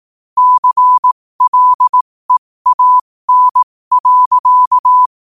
Morse code, heard as a keyed tone, about 18 wpm, spaced as for 15 wpm, 1000 Hz. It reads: CLEAN.